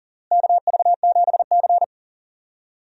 Morse code, heard as a keyed tone, about 40 wpm, 700 Hz. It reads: KV7C
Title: USA Call Signs - 4 Characters - Repeated 3x - 40wpm